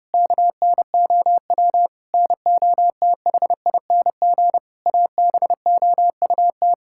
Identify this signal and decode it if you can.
Morse code, heard as a keyed tone, about 28 wpm, 700 Hz. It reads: KNOW NOTHING ABOUT
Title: Sentences from Top 1000 Words - Review Part 6 - 30wpm